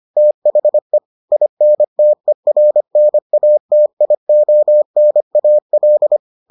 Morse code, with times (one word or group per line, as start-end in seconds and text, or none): 0.2-1.0 THE
1.3-6.2 INTERNATIONAL